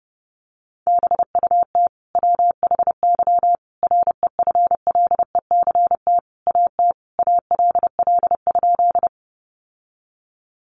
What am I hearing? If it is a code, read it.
Morse code, heard as a keyed tone, about 30 wpm, 700 Hz. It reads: BUT WHY REFLECT AT ALL?
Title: Sentences from Top 1500 Words 30wpm